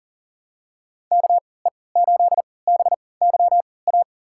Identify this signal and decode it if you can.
Morse code, heard as a keyed tone, about 40 wpm, spaced as for 23 wpm, 700 Hz. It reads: KE8BYA
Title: USA Call Signs - 6 Characters - 3x Character Spacing - 40wpm